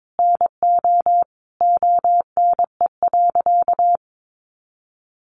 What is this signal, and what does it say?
Morse code, heard as a keyed tone, about 22 wpm, 700 Hz. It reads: NO ONE.
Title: Additional Easy Phrases 22wpm